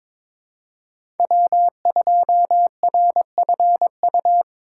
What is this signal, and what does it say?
Morse code, heard as a keyed tone, about 22 wpm, 700 Hz. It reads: W2RFU
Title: USA Call Signs - 5 Characters - 3x Character Spacing - 22wpm